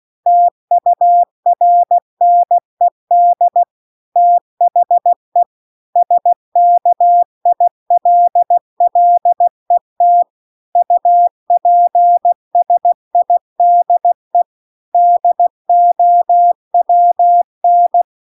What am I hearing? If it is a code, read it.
Morse code, heard as a keyed tone, about 16 wpm, 700 Hz. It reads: TURNED THE SKILLET UPSIDE DOWN